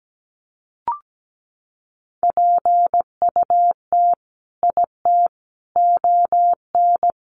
Morse code, heard as a keyed tone, about 17 wpm, 700 Hz. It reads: PUT IT ON
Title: Sentences from Top 1400 Words 17wpm